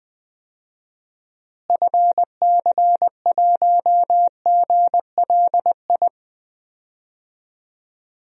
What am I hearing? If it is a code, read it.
Morse code, heard as a keyed tone, about 20 wpm, 700 Hz. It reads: FC1GLI
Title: DXCC-100 Call Signs - 6 Characters - Repeated 2x - 20wpm